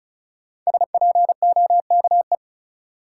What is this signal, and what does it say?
Morse code, heard as a keyed tone, about 34 wpm, 700 Hz. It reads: SPOKE
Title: Sentences from Top 1200 Words - New Word Drill Alt - 35wpm